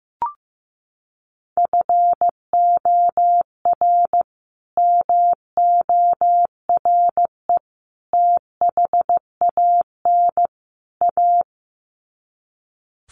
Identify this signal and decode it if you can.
Morse code, heard as a keyed tone, about 15 wpm, 700 Hz. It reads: FOR MORE THAN A
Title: Sets of 4 Words 15wpm